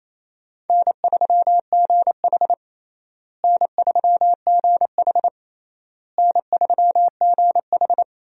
Morse code, heard as a keyed tone, about 28 wpm, 700 Hz. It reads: N3GH N3GH N3GH
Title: USA Call Signs - 4 Characters - Repeated 3x - 28wpm